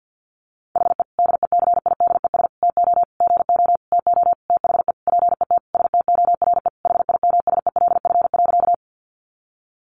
Morse code, heard as a keyed tone, about 50 wpm, 700 Hz. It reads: HE DECIDES TO GO TO THE PET STORE HIMSELF.
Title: Additional Easy Phrases 50wpm